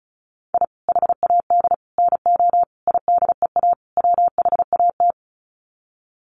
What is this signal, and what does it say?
Morse code, heard as a keyed tone, about 34 wpm, 700 Hz. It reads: I HAD NO IDEA WHAT
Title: Sets of 5 Words Encore 35wpm